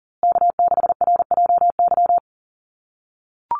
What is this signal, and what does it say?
Morse code, heard as a keyed tone, about 40 wpm, 700 Hz. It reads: K6RJY
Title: USA Call Signs Repeated 2x 40wpm